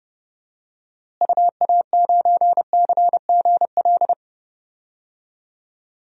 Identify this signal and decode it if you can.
Morse code, heard as a keyed tone, about 30 wpm, 700 Hz. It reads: UA9CGL
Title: DXCC-100 Call Signs - 6 Characters - Repeated 3x - 30wpm